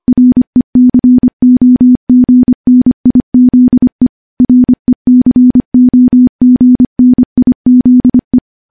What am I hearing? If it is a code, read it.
Morse code, heard as a keyed tone, about 25 wpm, 250 Hz. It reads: RECOGNIZE RECOGNIZE
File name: morse-training-25WPM